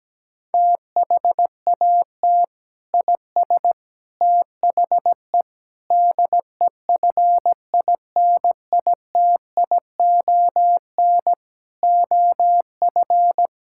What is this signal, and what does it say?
Morse code, heard as a keyed tone, about 17 wpm, 700 Hz. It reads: THAT IS THE DEFINITION OF
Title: Sentences from Top 1000 Words - Review Part 6 - 17wpm